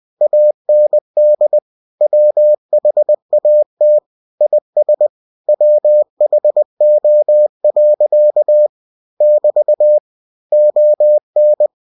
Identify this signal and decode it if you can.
Morse code, heard as a keyed tone, about 20 wpm, 600 Hz. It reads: AND WHAT IS WHO. <BT> ON
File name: cw-books / winnie-the-pooh / winnie-20wpm-page-0068